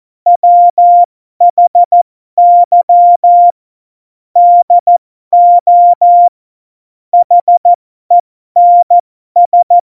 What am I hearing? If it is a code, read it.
Morse code, heard as a keyed tone, about 14 wpm, spaced as for 12 wpm, 700 Hz. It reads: WHY DO HENS
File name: Morsekurs / Witze